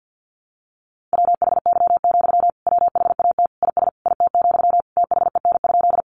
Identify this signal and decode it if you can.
Morse code, heard as a keyed tone, about 50 wpm, 700 Hz. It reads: WHY, WHAT IS IT, THEN?
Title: Sentences from Top 200 Words 50wpm